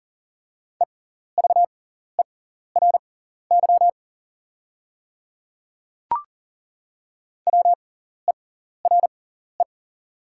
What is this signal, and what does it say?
Morse code, heard as a keyed tone, about 37 wpm, spaced as for 14 wpm, 700 Hz. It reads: EVERY WERE
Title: 6x Character Spacing - Top 300 Words - 40wpm